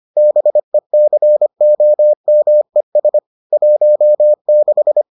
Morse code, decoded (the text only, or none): BECOMES 16